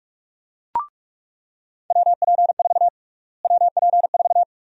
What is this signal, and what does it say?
Morse code, heard as a keyed tone, about 42 wpm, 700 Hz. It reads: WP4 WP4
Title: USA Call Sign Prefixes - Repeated 2x - 45wpm